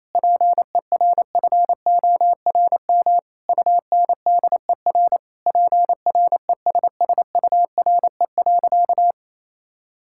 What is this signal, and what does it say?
Morse code, heard as a keyed tone, about 28 wpm, 700 Hz. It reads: PERFORM UNDER PRESSURE.
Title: Sentences from Top 1000 Words - Review Part 6 - 28wpm